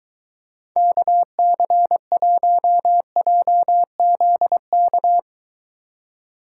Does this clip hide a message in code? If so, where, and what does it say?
Yes: Morse code from 0.8 to 5.2 s, heard as a keyed tone, about 23 wpm, 700 Hz. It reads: KC1JZK